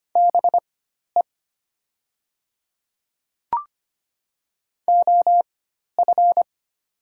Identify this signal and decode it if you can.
Morse code, heard as a keyed tone, about 25 wpm, spaced as for 12 wpm, 700 Hz. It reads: BE OF
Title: 4x Character Spacing - Top 100 Words - 25wpm